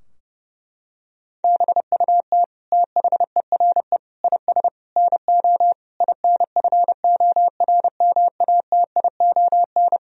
Morse code, heard as a keyed tone, about 32 wpm, 700 Hz. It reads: BUT THERE IS NO INFORMATION
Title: Sentences from Top 1000 Words - Review Part 3 - 30wpm